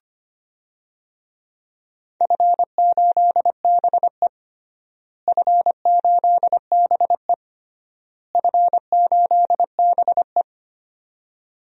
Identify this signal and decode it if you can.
Morse code, heard as a keyed tone, about 25 wpm, 700 Hz. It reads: F8BE F8BE F8BE
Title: DXCC-100 Call Signs - 4 Characters - Repeated 3x - 25wpm